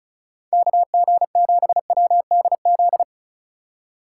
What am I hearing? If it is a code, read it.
Morse code, heard as a keyed tone, about 36 wpm, 700 Hz. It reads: KG7WDZ
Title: USA Call Signs - 6 Characters - 8x Character Spacing - 35wpm